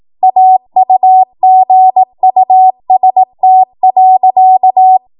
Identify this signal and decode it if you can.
Morse code, heard as a keyed tone, about 18 wpm, 750 Hz. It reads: AUGUST.
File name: Morse / 18WPM